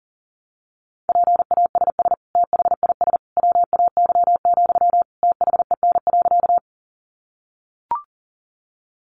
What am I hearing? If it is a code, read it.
Morse code, heard as a keyed tone, about 40 wpm, 700 Hz. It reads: PASS THIS WAY, THEN.
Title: Sentences from Top 400 Words 40wpm